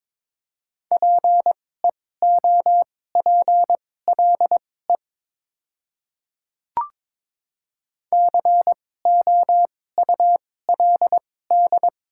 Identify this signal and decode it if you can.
Morse code, heard as a keyed tone, about 22 wpm, spaced as for 16 wpm, 700 Hz. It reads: PEOPLE COULD
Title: 2x Character Spacing - Top 100 Words - 22wpm